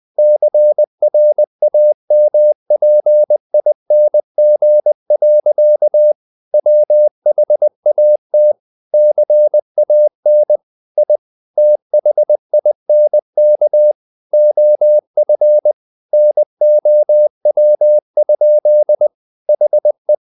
Morse code, heard as a keyed tone, about 20 wpm, 600 Hz. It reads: CRAMPING. WHAT CAN I THINK OF NOW? HE
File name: the old man and the sea part 2 20wpm